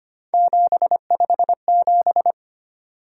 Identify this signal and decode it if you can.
Morse code, heard as a keyed tone, about 25 wpm, 700 Hz. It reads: 757
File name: Three Letter-Number Combinations - 25wpm